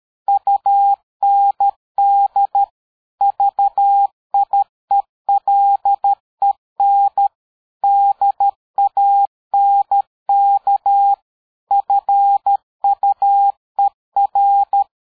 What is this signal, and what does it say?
Morse code, heard as a keyed tone, about 13 wpm, 800 Hz. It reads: UND VIELEN DANK FUER
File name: dk5ke / PersWorte 60